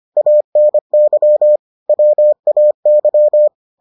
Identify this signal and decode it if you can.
Morse code, heard as a keyed tone, about 25 wpm, 600 Hz. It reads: ANY WAY